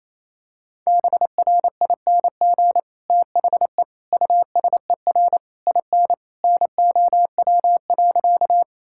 Morse code, heard as a keyed tone, about 27 wpm, 700 Hz. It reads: BRING THE USER IN NOW.
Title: Sentences from Top 1000 Words - Review Part 5 - 28wpm